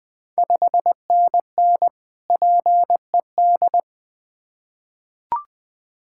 Morse code, heard as a keyed tone, about 20 wpm, 690 Hz. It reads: 5NN PED
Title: Ontario QSO Party Exchange 20wpm